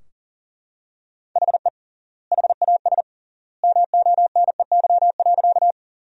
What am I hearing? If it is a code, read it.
Morse code, heard as a keyed tone, about 40 wpm, 700 Hz. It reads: HE HAS MONEY.
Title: 3x Word-Spacing - Sentences from Top 400 Words - 40wpm